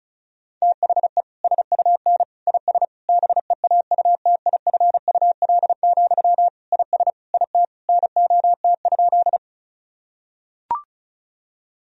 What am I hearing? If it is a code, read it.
Morse code, heard as a keyed tone, about 36 wpm, 700 Hz. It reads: THE SUN IS BEAUTIFUL, IS IT NOT?
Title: Sentences from Top 1000 Words - Review Part 1 - 35wpm